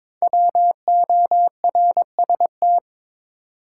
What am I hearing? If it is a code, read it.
Morse code, heard as a keyed tone, about 22 wpm, 700 Hz. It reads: WORST
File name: 5 Letter Words - First 3 Letters Given - 22wpm